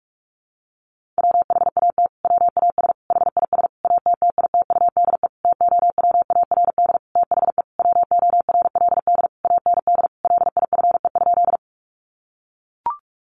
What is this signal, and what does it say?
Morse code, heard as a keyed tone, about 44 wpm, 700 Hz. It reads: WHAT WAS HIS ATTITUDE TOWARD THE WORLD AND LIFE?
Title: Sentences from Top 1300 Words 45wpm